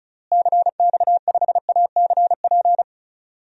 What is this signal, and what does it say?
Morse code, heard as a keyed tone, about 34 wpm, 700 Hz. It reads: CX5ACP
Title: DXCC-100 Call Signs - 6 Characters - Repeated 2x - 35wpm